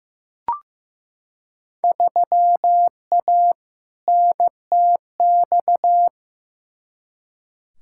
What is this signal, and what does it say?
Morse code, heard as a keyed tone, about 15 wpm, 700 Hz. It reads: 3A NTX
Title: Field Day Exchanges - 15wpm